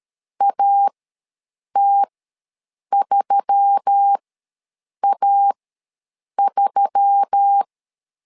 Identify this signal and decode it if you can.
Morse code, heard as a keyed tone, about 13 wpm, spaced as for 7 wpm, 800 Hz. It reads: AT3A3
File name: LEZIONI / 4 Q1S3 ws 12wpm 2038